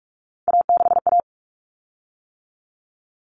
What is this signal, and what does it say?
Morse code, heard as a keyed tone, about 44 wpm, 700 Hz. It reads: A6A
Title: Common Cut Numbers - 3-Digit 45wpm